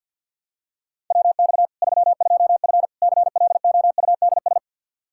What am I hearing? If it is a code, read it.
Morse code, heard as a keyed tone, about 48 wpm, 700 Hz. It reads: WX 32F CLOUDS
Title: Practice QSO - Exchange 2 - Info - 50wpm